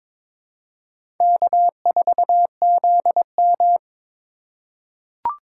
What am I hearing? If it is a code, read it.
Morse code, heard as a keyed tone, about 22 wpm, 700 Hz. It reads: K4ZM
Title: USA Call Signs - 2x Character Spacing - 22wpm